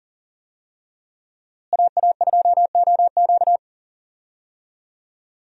Morse code, heard as a keyed tone, about 40 wpm, 700 Hz. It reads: AA2OQ
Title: USA Call Signs - 5 Characters - Repeated 2x - 40wpm